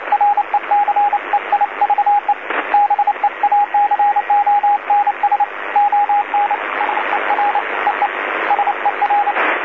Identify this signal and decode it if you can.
Morse code, heard as a keyed tone, about 28 wpm, 820 Hz, with radio noise. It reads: RECEIVE BEACONS ON SER EE SER
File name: DC-TRCVR / 80-CW